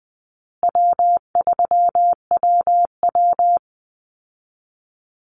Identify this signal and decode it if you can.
Morse code, heard as a keyed tone, about 20 wpm, 700 Hz. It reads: W3WW
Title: Call Signs Repeated 3x - 20wpm